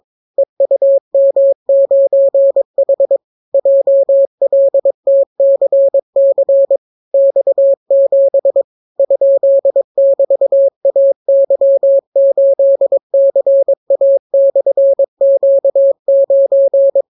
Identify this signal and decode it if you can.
Morse code, heard as a keyed tone, about 22 wpm, 550 Hz. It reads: EUM9H JLTCC X7 ?<BT>AY8CA/Q9